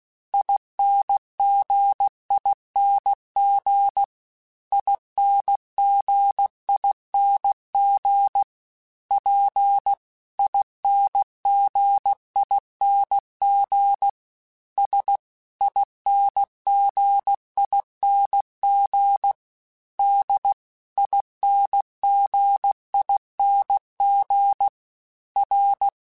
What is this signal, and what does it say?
Morse code, heard as a keyed tone, about 16 wpm, 800 Hz. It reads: INGING INGING P INGING S INGING D INGING R